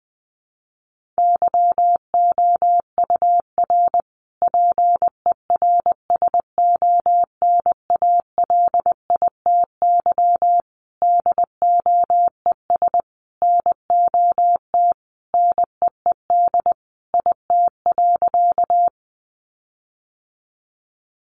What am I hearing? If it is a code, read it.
Morse code, heard as a keyed tone, about 20 wpm, 700 Hz. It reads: YOUR PERSONALITY DOES NOT NEED IT.
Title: Sentences from Top 1400 Words 20wpm